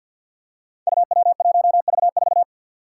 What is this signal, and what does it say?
Morse code, heard as a keyed tone, about 47 wpm, 700 Hz. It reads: UW1V4
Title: Five Challenging Letter-Number Combinations 50wpm